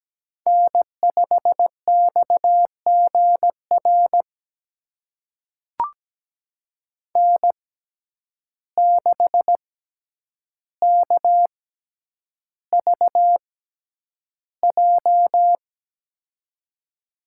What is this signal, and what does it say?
Morse code, heard as a keyed tone, about 17 wpm, spaced as for 9 wpm, 700 Hz. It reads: N5XGR N6KVJ
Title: USA Call Signs - 5 Characters - 6x Character Spacing - 17wpm